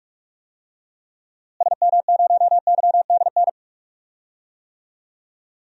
Morse code, heard as a keyed tone, about 46 wpm, 700 Hz. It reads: SM0YDN
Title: DXCC-100 Call Signs - 6 Characters - Repeated 2x - 45wpm